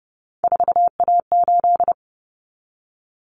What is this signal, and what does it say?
Morse code, heard as a keyed tone, about 30 wpm, 700 Hz. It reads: 4A8
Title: Common Cut Numbers - 3-Digit 30wpm